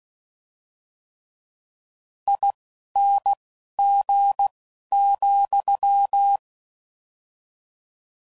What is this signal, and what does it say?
Morse code, heard as a keyed tone, about 16 wpm, spaced as for 11 wpm, 800 Hz. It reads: ING,
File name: Suffixes / General / ESE INGs